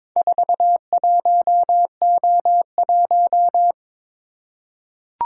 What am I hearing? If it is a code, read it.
Morse code, heard as a keyed tone, about 22 wpm, 700 Hz. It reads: 41O1
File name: practice / 22wpm / Four Letter-Number Combinations - 22wpm